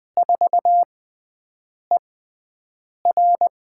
Morse code, heard as a keyed tone, about 20 wpm, spaced as for 7 wpm, 700 Hz. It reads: 4ER
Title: USA Call Signs - 4 Characters - 6x Character Spacing - 20wpm